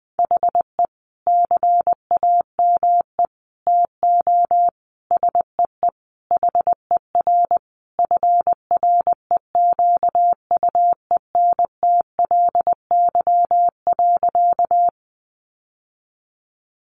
Morse code, heard as a keyed tone, about 20 wpm, 700 Hz. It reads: HE CAME TO SEE HER FREQUENTLY.
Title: Sentences from Top 1300 Words 20wpm